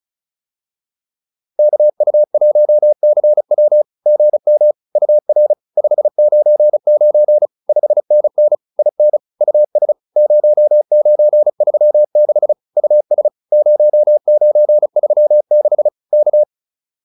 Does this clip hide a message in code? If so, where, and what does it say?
Yes: Morse code from 1.6 to 16.4 s, heard as a keyed tone, about 34 wpm, 600 Hz. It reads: KU1CW GM UR 599 5NN IN US 0936 US 0936 K